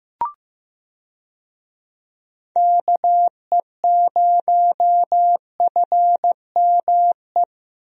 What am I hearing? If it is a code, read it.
Morse code, heard as a keyed tone, about 15 wpm, 700 Hz. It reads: KE0FME
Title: USA Call Signs - 6 Characters - Repeated 3x - 15wpm